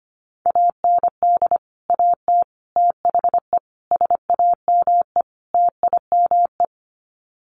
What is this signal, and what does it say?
Morse code, heard as a keyed tone, about 25 wpm, 700 Hz. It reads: AND AT THE SAME TIME
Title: Sets of 5 Words 25wpm